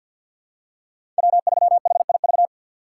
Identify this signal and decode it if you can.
Morse code, heard as a keyed tone, about 47 wpm, 700 Hz. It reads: W3HIV
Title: USA Call Signs - 5 Characters - 6x Character Spacing - 50wpm